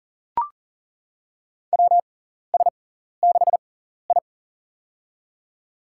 Morse code, heard as a keyed tone, about 40 wpm, spaced as for 14 wpm, 700 Hz. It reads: WS6I
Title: USA Call Signs - 4 Characters - 6x Character Spacing - 40wpm